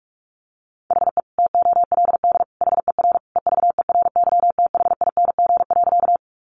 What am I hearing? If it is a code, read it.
Morse code, heard as a keyed tone, about 44 wpm, 700 Hz. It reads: HE TOLD HER EVERYTHING.
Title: Sentences from Top 700 Words 45wpm